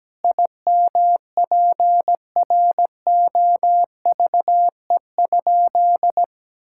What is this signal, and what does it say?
Morse code, heard as a keyed tone, about 17 wpm, 700 Hz. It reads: IMPROVE?
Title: Sentences from Top 1000 Words - Review Part 5 - 17wpm